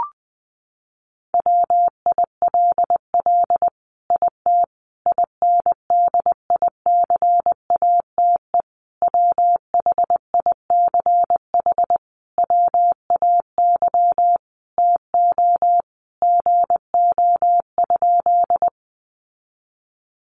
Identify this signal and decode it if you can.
Morse code, heard as a keyed tone, about 20 wpm, 700 Hz. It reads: WILL IT INDICATE WHICH WAY TO GO?
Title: Sentences from Top 1300 Words 20wpm